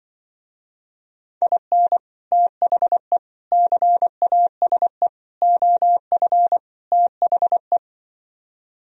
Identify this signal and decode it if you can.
Morse code, heard as a keyed tone, about 24 wpm, 700 Hz. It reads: IN THE CASE OF THE
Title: Speed-Racing - Sets of 5 Words 20wpm